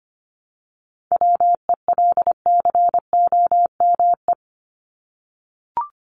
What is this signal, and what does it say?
Morse code, heard as a keyed tone, about 25 wpm, 700 Hz. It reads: WELCOME
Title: Top 1751-2000 Words 25wpm